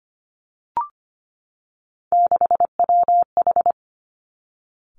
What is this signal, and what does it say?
Morse code, heard as a keyed tone, about 25 wpm, 700 Hz. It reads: 6WH
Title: Three Challenging Letter-Number Combinations 25wpm